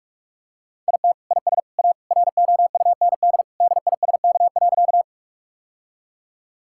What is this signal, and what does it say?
Morse code, heard as a keyed tone, about 46 wpm, 700 Hz. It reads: IT IS A ROUND DISK.
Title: Sentences from Top 1000 Words - Review Part 6 - 45wpm